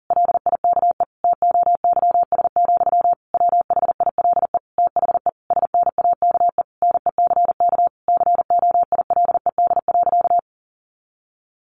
Morse code, heard as a keyed tone, about 40 wpm, 700 Hz. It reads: LIKE TOYS, WHILE THE SNAKE NECK COILED.
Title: A Sound Of Thunder 40wpm